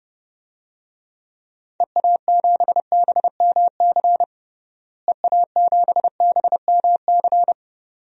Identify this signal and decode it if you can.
Morse code, heard as a keyed tone, about 30 wpm, 700 Hz. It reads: EA7BMC EA7BMC
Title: DXCC-100 Call Signs - 6 Characters - Repeated 2x - 30wpm